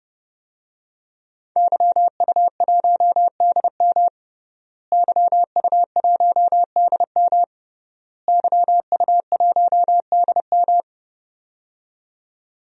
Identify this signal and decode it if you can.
Morse code, heard as a keyed tone, about 32 wpm, 700 Hz. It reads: YU1DM YU1DM YU1DM
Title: DXCC-100 Call Signs - 5 Characters - Repeated 3x - 30wpm